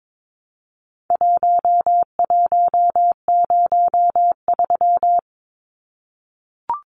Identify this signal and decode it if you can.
Morse code, heard as a keyed tone, about 22 wpm, 700 Hz. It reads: 1103
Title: Numbers - Contest Style - 4-Digit 22wpm